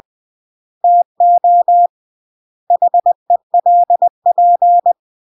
Morse code, heard as a keyed tone, about 20 wpm, 700 Hz. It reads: TO HELP
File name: CW / jodorowsky / 82 commandments 20-20